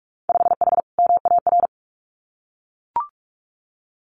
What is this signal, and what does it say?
Morse code, heard as a keyed tone, about 44 wpm, 700 Hz. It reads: 5H MAR
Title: Winter Field Day Exchanges Encore 45wpm